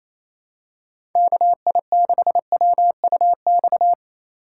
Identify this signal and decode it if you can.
Morse code, heard as a keyed tone, about 28 wpm, 700 Hz. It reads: KI6WUX
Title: USA Call Signs - 2x Character Spacing - 28wpm